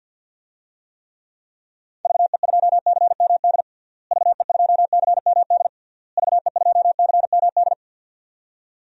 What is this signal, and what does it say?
Morse code, heard as a keyed tone, about 52 wpm, 700 Hz. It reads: VE2CMD VE2CMD VE2CMD
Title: DXCC-100 Call Signs - 6 Characters - Repeated 3x - 50wpm